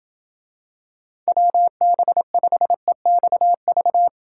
Morse code, heard as a keyed tone, about 26 wpm, 700 Hz. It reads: WB5EXV